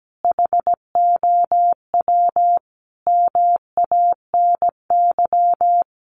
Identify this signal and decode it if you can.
Morse code, heard as a keyed tone, about 17 wpm, 700 Hz. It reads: HOW MANY
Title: Sets of 5 Words Encore 17wpm